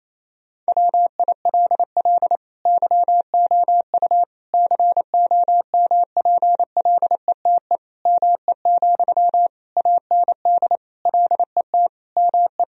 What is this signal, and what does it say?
Morse code, heard as a keyed tone, about 28 wpm, 700 Hz. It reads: WILL YOU COMPLETE ME, AND LET ME
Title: Sentences from Top 1000 Words - Review Part 2 - 28wpm